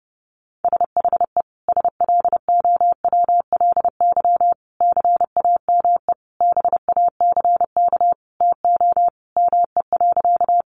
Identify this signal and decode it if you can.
Morse code, heard as a keyed tone, about 30 wpm, 700 Hz. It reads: SHE SLOWLY CAME BACK TO ME.
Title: Sentences from Top 1100 Words 30wpm